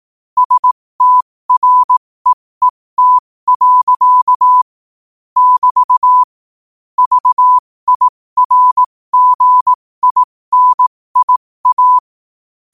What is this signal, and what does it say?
Morse code, heard as a keyed tone, about 18 wpm, spaced as for 15 wpm, 1000 Hz. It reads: STREET. <BT> VIRGINIA
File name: qotdfiles / 15wpm